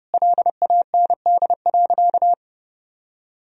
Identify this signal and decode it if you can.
Morse code, heard as a keyed tone, about 30 wpm, 700 Hz. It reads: LAND.
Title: Sentences from Top 1000 Words - Review Part 3 - 30wpm